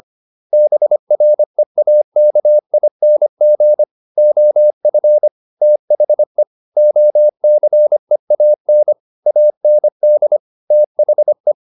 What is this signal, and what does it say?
Morse code, heard as a keyed tone, about 25 wpm, 600 Hz. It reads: BREAKING OF THE OCEAN AND THE